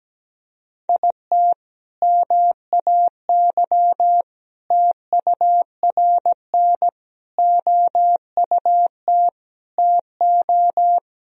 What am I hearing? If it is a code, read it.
Morse code, heard as a keyed tone, about 17 wpm, 700 Hz. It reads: IT MAY TURN OUT TO
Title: Sentences from Top 400 Words and Abbreviations 17wpm